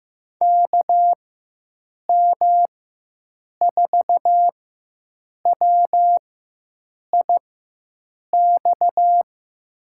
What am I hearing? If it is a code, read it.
Morse code, heard as a keyed tone, about 16 wpm, spaced as for 7 wpm, 700 Hz. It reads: KM4WIX
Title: USA Call Signs - 6 Characters - 4x Character Spacing - 15wpm